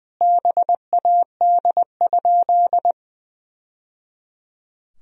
Morse code, heard as a keyed tone, about 20 wpm, 700 Hz. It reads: BAD?
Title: Sentences from Top 1000 Words - Review Part 1 - 20wpm